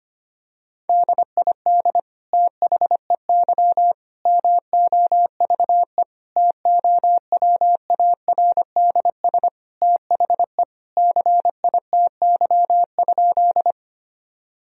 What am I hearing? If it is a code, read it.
Morse code, heard as a keyed tone, about 25 wpm, 700 Hz. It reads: DID THEY MOVE TOWARDS THE CITY?
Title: Sentences from Top 1000 Words - Review Part 2 - 25wpm